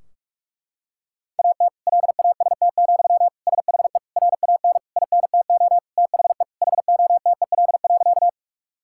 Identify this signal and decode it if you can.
Morse code, heard as a keyed tone, about 46 wpm, 700 Hz. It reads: AT LAST, SHE RAN INTO THE HOTEL.
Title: Sentences from Top 1600 Words 45wpm